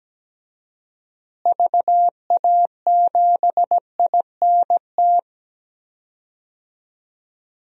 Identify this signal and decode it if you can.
Morse code, heard as a keyed tone, about 17 wpm, 700 Hz. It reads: VA7INT
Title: DXCC-100 Call Signs - 6 Characters - Repeated 3x - 17wpm